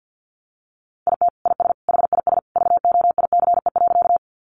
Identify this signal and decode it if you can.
Morse code, heard as a keyed tone, about 50 wpm, 700 Hz. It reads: IT IS HIS VOICE.
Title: Sentences from Top 500 Words 50wpm